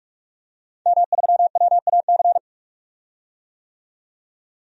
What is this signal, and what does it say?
Morse code, heard as a keyed tone, about 41 wpm, 700 Hz. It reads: M3WAC
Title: DXCC-100 Call Signs - 5 Characters - 45wpm